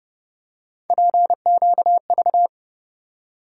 Morse code, heard as a keyed tone, about 29 wpm, 700 Hz. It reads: PQV
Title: USA Call Sign Postfixes - 3 Letters 30wpm